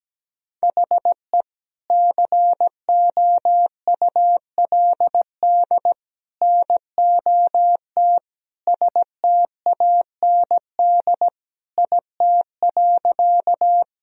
Morse code, heard as a keyed tone, about 17 wpm, 700 Hz. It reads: HE COULD NOT STAND IT.
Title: Sentences from Top 1000 Words - Review Part 1 - 17wpm